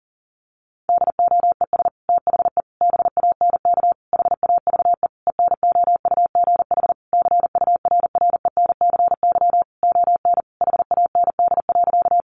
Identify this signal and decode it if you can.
Morse code, heard as a keyed tone, about 40 wpm, 700 Hz. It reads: DOES THE BANK HAVE ENOUGH CURRENCY ON HAND.
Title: Sentences from Top 1500 Words 40wpm